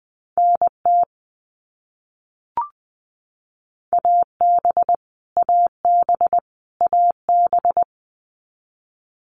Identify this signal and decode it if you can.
Morse code, heard as a keyed tone, about 20 wpm, 700 Hz. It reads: NT AB AB AB
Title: Canadian Provinces and Territories - Repeated 3x - 20wpm